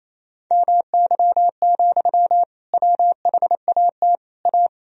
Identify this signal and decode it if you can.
Morse code, heard as a keyed tone, about 28 wpm, 700 Hz. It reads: MY, WHAT A
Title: Sentences from Top 1000 Words - Review Part 2 - 28wpm